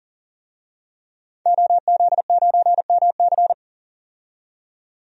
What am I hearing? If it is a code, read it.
Morse code, heard as a keyed tone, about 37 wpm, 700 Hz. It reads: OZ9MC
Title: DXCC-100 Call Signs - 5 Characters - 40wpm